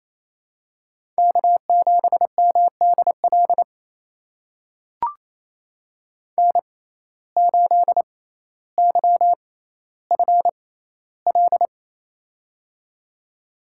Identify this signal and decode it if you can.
Morse code, heard as a keyed tone, about 28 wpm, spaced as for 14 wpm, 700 Hz. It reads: K7MDL N8YFL
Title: USA Call Signs - 5 Characters - 6x Character Spacing - 28wpm